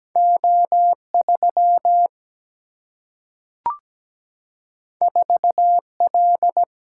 Morse code, heard as a keyed tone, about 17 wpm, 700 Hz. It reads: O3 4L
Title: Single-Letter Single-Number 17wpm